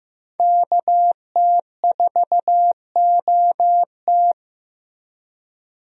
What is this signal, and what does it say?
Morse code, heard as a keyed tone, about 15 wpm, 700 Hz. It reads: KT4OT
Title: USA Call Signs - 5 Characters - 15wpm